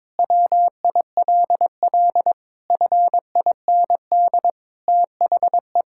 Morse code, heard as a keyed tone, about 22 wpm, 700 Hz. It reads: WILL FIND THE
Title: Sentences from Top 1000 Words - Review Part 6 - 22wpm